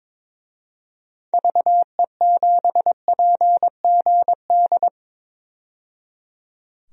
Morse code, heard as a keyed tone, about 22 wpm, 700 Hz. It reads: VE7PGD
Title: DXCC-100 Call Signs - 6 Characters - 22wpm